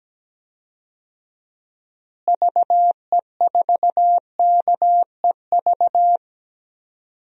Morse code, heard as a keyed tone, about 17 wpm, 700 Hz. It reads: VE4KEV
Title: DXCC-100 Call Signs - 6 Characters - Repeated 3x - 17wpm